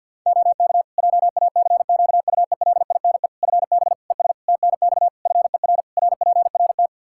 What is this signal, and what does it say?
Morse code, heard as a keyed tone, about 51 wpm, 700 Hz. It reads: OK JACQUELINE FB ES TNX FER RPRT
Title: K1USN SST QSO Practice 50wpm